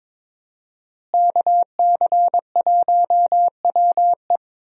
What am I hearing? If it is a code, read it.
Morse code, heard as a keyed tone, about 22 wpm, 700 Hz. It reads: KC1WE
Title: USA Call Signs - 5 Characters - Repeated 2x - 22wpm